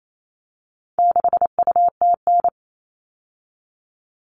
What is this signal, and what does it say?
Morse code, heard as a keyed tone, about 28 wpm, 700 Hz. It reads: 6UTN
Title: All Cut Numbers - 4-Digit 28wpm